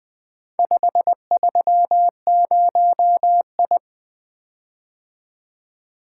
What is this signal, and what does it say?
Morse code, heard as a keyed tone, about 24 wpm, 700 Hz. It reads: 530I
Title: Speed-Racing - Four Letter-Number Combinations - 20wpm